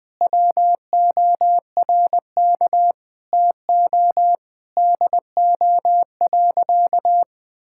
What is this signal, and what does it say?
Morse code, heard as a keyed tone, about 19 wpm, 700 Hz. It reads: WORK TO DO.
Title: Sentences with 8-Word Phrases - Spoken Beginning - 20wpm